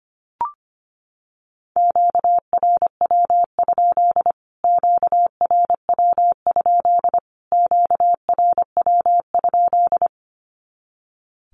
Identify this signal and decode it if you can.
Morse code, heard as a keyed tone, about 25 wpm, 700 Hz. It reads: QRW? QRW? QRW?
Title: Q-Codes Repeated 3x 25wpm